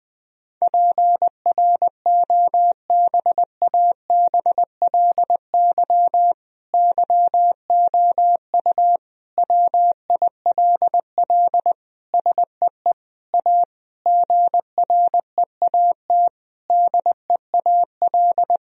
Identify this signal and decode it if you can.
Morse code, heard as a keyed tone, about 20 wpm, 700 Hz. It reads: PROBABLY YOU WILL SEE A GREAT DEAL
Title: Sentences from Top 1000 Words - Review Part 4 - 20wpm